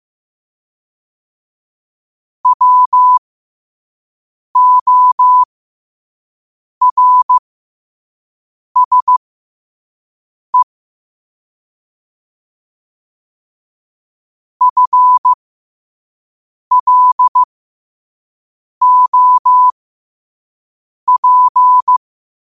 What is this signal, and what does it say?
Morse code, heard as a keyed tone, about 15 wpm, spaced as for 5 wpm, 1000 Hz. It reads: WORSE FLOP
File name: qotdfiles / 5wpm